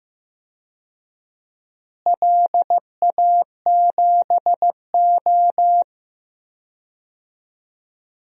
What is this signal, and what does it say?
Morse code, heard as a keyed tone, about 16 wpm, 700 Hz. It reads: LA7O
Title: DXCC-100 Call Signs - 4 Characters - Repeated 3x - 15wpm